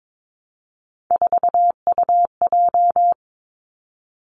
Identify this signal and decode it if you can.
Morse code, heard as a keyed tone, about 22 wpm, 700 Hz. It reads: 4UJ
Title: Three Challenging Letter-Number Combinations 22wpm